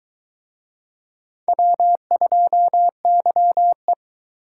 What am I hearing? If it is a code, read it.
Morse code, heard as a keyed tone, about 23 wpm, 700 Hz. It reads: W2YE